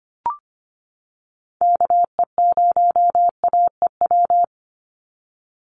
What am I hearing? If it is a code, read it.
Morse code, heard as a keyed tone, about 25 wpm, 700 Hz. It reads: KE0AEW
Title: USA Call Signs 25wpm